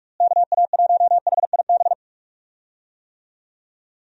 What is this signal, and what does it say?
Morse code, heard as a keyed tone, about 46 wpm, 700 Hz. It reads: KA1HIB
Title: USA Call Signs - 6 Characters - Repeated 2x - 45wpm